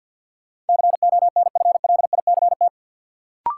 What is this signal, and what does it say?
Morse code, heard as a keyed tone, about 53 wpm, 700 Hz. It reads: CONFLICT
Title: Sentences from Top 1300 Words - New Word Drill Alt - 50wpm